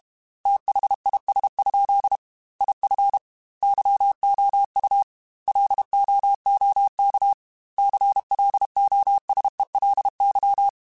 Morse code, heard as a keyed tone, about 32 wpm, 800 Hz. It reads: THIS? IF YOU LOOK CLOSELY
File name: Technical Subjects / Technical Subjects-mp3s / OR Gate @30